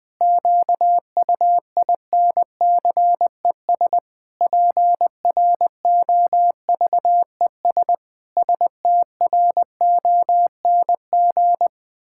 Morse code, heard as a keyed tone, about 20 wpm, 700 Hz. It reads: QUINCES PROVES STRONG
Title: Pangram Challenge v2 - 20wpm